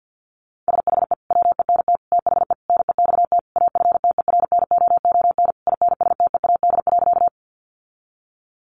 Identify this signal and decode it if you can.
Morse code, heard as a keyed tone, about 50 wpm, 700 Hz. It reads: SHE WENT THE NEXT AFTERNOON INSTEAD.
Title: Sentences from Top 1500 Words 50wpm